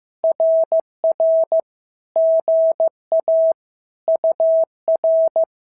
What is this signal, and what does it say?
Morse code, heard as a keyed tone, about 14 wpm, 650 Hz. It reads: RR GA UR
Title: SOTA QSO Practice 15wpm